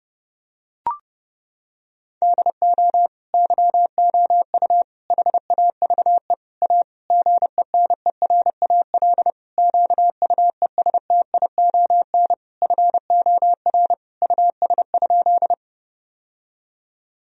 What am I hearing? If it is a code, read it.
Morse code, heard as a keyed tone, about 30 wpm, 700 Hz. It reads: DO YOU HAVE A GENERAL QUESTION FOR US?
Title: Sentences from Top 1000 Words - Review Part 3 - 30wpm